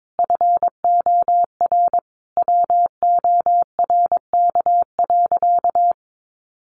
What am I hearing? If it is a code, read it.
Morse code, heard as a keyed tone, about 22 wpm, 700 Hz. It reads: FOR WORK.
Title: Sentences from Top 700 Words 22wpm